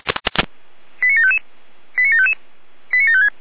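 Five-tone selective call (ZVEI), heard as quick runs of five tones, tones of about 70 ms. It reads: 89755 89755 89756
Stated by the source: ZVEI five-tone call received by SDR, 455.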